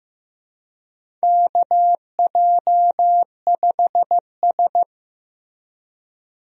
Morse code, heard as a keyed tone, about 15 wpm, 700 Hz. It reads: KJ5S
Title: USA Call Signs - 4 Characters - Repeated 2x - 15wpm